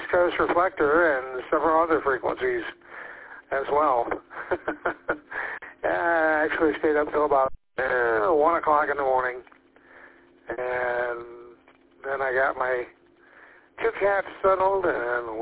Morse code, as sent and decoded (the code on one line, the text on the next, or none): none